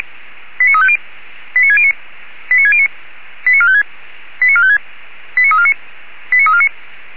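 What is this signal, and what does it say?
Five-tone selective call (ZVEI), heard as quick runs of five tones, tones of about 70 ms. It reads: 89350 89609 89609 89456 89456 89349 89349
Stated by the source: ZVEI five-tone call received by SDR, logged by Feuerwehr Orlamünde (fire brigade).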